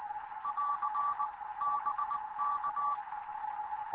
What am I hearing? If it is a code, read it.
Morse code, heard as a keyed tone, about 20 wpm, 1160 Hz, with radio noise. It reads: <AR> BK
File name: Morze-HA2MN / hangp / 2 11 GYAKORLAT